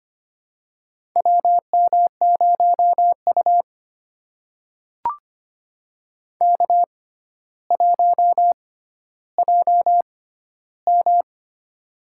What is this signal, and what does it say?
Morse code, heard as a keyed tone, about 25 wpm, spaced as for 13 wpm, 700 Hz. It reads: WM0U K1JM
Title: USA Call Signs - 4 Characters - 6x Character Spacing - 25wpm